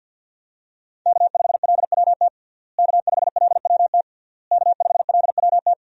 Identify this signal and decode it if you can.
Morse code, heard as a keyed tone, about 46 wpm, 700 Hz. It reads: K5LWT K5LWT K5LWT
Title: USA Call Signs - 5 Characters - Repeated 3x - 50wpm